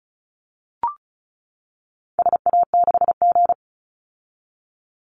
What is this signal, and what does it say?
Morse code, heard as a keyed tone, about 34 wpm, 700 Hz. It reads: SA6G
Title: Call Signs - 35wpm